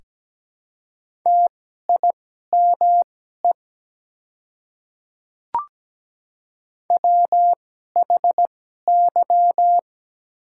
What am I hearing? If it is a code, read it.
Morse code, heard as a keyed tone, about 17 wpm, spaced as for 12 wpm, 700 Hz. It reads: TIME WHY
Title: 2x Character Spacing - Top 300 Words - 17wpm